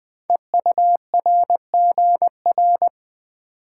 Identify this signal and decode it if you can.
Morse code, heard as a keyed tone, about 20 wpm, 700 Hz. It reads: EURGR
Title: Code Groups - 5 Letters - 20wpm